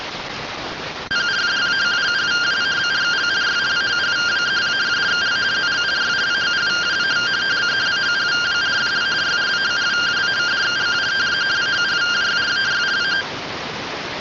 Two-tone teleprinter signal, about 170 Hz shift.